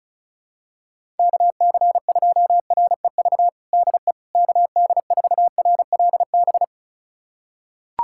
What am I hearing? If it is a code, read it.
Morse code, heard as a keyed tone, about 36 wpm, 700 Hz. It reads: KC2REV DE KD4RLB
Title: USA Callsign DE Callsign - 6 Characters - 35wpm